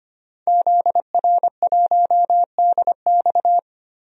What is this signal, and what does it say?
Morse code, heard as a keyed tone, about 25 wpm, 700 Hz. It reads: ZR1DX
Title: DXCC-100 Call Signs - 5 Characters - Repeated 3x - 25wpm